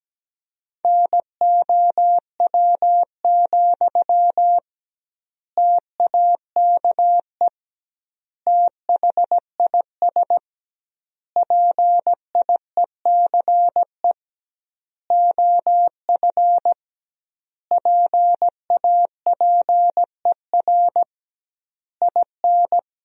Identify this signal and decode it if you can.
Morse code, heard as a keyed tone, about 17 wpm, 700 Hz. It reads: NOW, TAKE THIS PIECE OF PAPER IN
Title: 2x Word-Spacing - Sentences from Top 400 Words - 17wpm